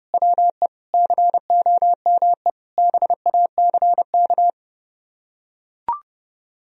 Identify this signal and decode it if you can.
Morse code, heard as a keyed tone, about 30 wpm, 700 Hz. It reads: WE COME BACK